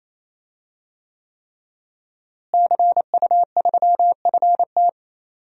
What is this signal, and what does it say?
Morse code, heard as a keyed tone, about 28 wpm, 700 Hz. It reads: CU3FT
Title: DXCC-100 Call Signs - 5 Characters - Repeated 3x - 28wpm